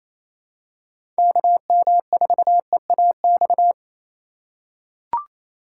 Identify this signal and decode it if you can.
Morse code, heard as a keyed tone, about 28 wpm, 700 Hz. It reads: KM4EAX
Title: USA Call Signs - 6 Characters - 4x Character Spacing - 28wpm